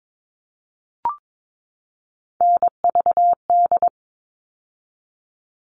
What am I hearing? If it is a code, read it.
Morse code, heard as a keyed tone, about 22 wpm, 700 Hz. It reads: NVD
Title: All Cut Numbers - 3-Digit 22wpm